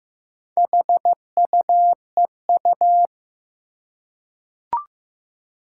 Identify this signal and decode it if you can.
Morse code, heard as a keyed tone, about 15 wpm, 700 Hz. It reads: HUEU
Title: Four Challenging Letter-Number Combinations 15wpm